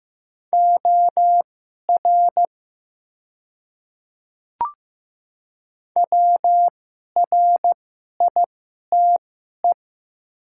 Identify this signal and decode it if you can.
Morse code, heard as a keyed tone, about 15 wpm, spaced as for 11 wpm, 700 Hz. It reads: OR WRITE
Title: 2x Character Spacing - Top 100 Words - 15wpm